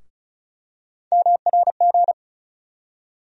Morse code, heard as a keyed tone, about 36 wpm, 700 Hz. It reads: MRG
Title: USA Call Sign Postfixes - 3 Letters 35wpm